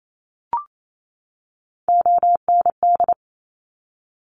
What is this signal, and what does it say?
Morse code, heard as a keyed tone, about 28 wpm, 700 Hz. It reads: OND
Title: Top Three Letter Combinations in Words 28wpm